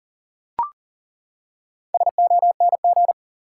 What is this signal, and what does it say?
Morse code, heard as a keyed tone, about 40 wpm, 700 Hz. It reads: SONG
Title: Four Letter Words - 40wpm